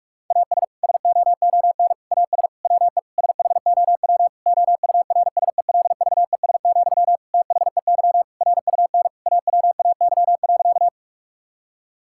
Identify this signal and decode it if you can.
Morse code, heard as a keyed tone, about 41 wpm, 700 Hz. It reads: AS SOON AS WE SHOW OURSELVES, THEY RUN AWAY.
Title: Sentences from Top 1600 Words 45wpm